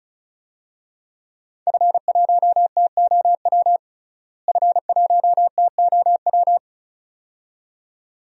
Morse code, heard as a keyed tone, about 36 wpm, 700 Hz. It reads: F1TOW F1TOW
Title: DXCC-100 Call Signs - 5 Characters - Repeated 2x - 35wpm